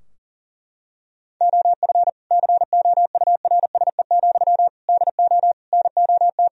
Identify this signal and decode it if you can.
Morse code, heard as a keyed tone, about 40 wpm, 700 Hz. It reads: OF COURSE, DO NOT
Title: Sentences from Top 1000 Words - Review Part 3 - 40wpm